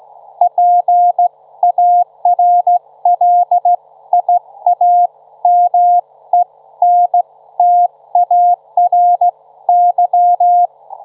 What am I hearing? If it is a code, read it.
Morse code, heard as a keyed tone, about 15 wpm, spaced as for 13 wpm, 710 Hz. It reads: PARLIAMENTARY
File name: Trivia-ZL2WT